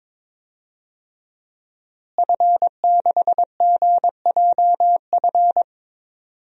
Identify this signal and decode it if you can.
Morse code, heard as a keyed tone, about 22 wpm, 700 Hz. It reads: F6GJF
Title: DXCC-100 Call Signs - 5 Characters - Repeated 3x - 22wpm